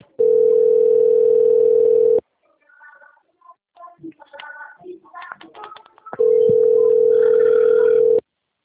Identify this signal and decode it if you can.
Ringback tone (North America), 440 + 480 Hz.
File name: out- - - - - .71734